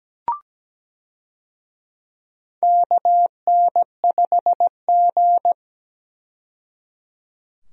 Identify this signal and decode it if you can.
Morse code, heard as a keyed tone, about 17 wpm, 700 Hz. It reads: KN5G
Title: USA Call Signs - 4 Characters - 17wpm